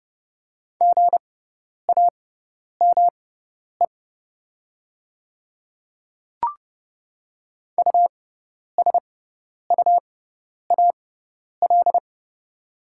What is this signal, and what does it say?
Morse code, heard as a keyed tone, about 30 wpm, spaced as for 10 wpm, 690 Hz. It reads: GAME USUAL
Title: 6x Character Spacing - Top 500 Words - 30wpm